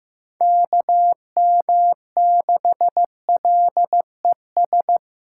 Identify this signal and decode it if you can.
Morse code, heard as a keyed tone, about 15 wpm, 700 Hz. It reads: KM6LES
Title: USA Callsign DE Callsign - 6 Characters - 15wpm